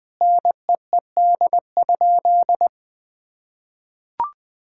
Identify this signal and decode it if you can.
Morse code, heard as a keyed tone, about 20 wpm, 700 Hz. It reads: NEED?
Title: Sentences from Top 1000 Words - Review Part 5 - 20wpm